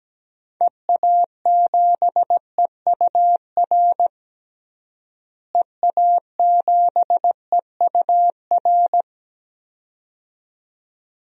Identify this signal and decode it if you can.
Morse code, heard as a keyed tone, about 17 wpm, 700 Hz. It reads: EA7EUR EA7EUR
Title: DXCC-100 Call Signs - 6 Characters - Repeated 2x - 17wpm